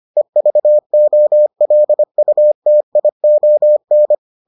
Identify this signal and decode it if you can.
Morse code, heard as a keyed tone, about 25 wpm, 600 Hz. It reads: EVOLUTION